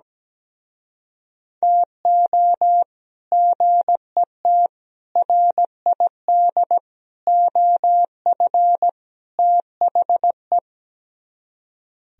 Morse code, heard as a keyed tone, about 17 wpm, 700 Hz. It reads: TO GET RID OF THE
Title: Speed-Racing - Sets of 5 Words 17wpm